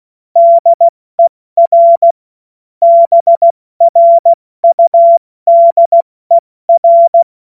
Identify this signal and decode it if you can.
Morse code, heard as a keyed tone, about 16 wpm, 680 Hz. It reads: DER BRUDER
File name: Morsekurs / Witze / Witz 24. August 2021, 70 F80